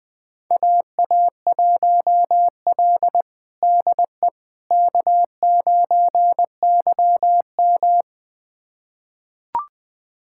Morse code, heard as a keyed tone, about 20 wpm, 700 Hz. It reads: AA1L DE K9YM
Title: USA Callsign DE Callsign - 4 Characters - 20wpm